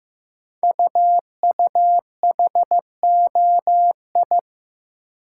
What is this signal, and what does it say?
Morse code, heard as a keyed tone, about 15 wpm, 700 Hz. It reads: UUHOI
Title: Code Groups - 5 Letters - 15wpm